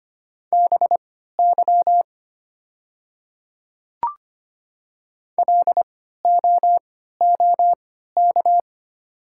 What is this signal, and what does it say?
Morse code, heard as a keyed tone, about 25 wpm, spaced as for 14 wpm, 700 Hz. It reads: BY LOOK
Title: 3x Character Spacing - Top 100 Words - 25wpm